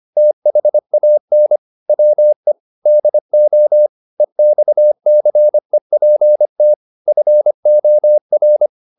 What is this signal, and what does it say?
Morse code, heard as a keyed tone, about 25 wpm, 600 Hz. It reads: THAN WE DO EXCEPT FOR
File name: the old man and the sea part 1 25wpm